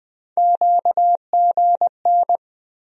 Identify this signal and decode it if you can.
Morse code, heard as a keyed tone, about 20 wpm, 700 Hz. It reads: QGN
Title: USA Call Sign Postfixes - 3 Letters 20wpm